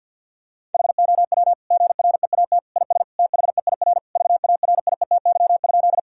Morse code, heard as a keyed tone, about 50 wpm, 700 Hz. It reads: HOW GREAT IS THEIR VARIETY?
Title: Sentences from Top 1000 Words - Review Part 4 - 50wpm